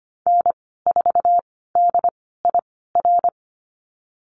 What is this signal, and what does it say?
Morse code, heard as a keyed tone, about 25 wpm, spaced as for 16 wpm, 700 Hz. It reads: N4DIR
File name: Lesson 15g - Number 5 - Call Signs 25wpm